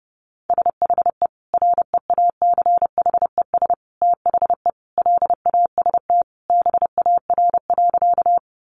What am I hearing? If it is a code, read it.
Morse code, heard as a keyed tone, about 28 wpm, 700 Hz. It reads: SHE REACHES THE LAST BAR.
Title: Easy Phrases 30wpm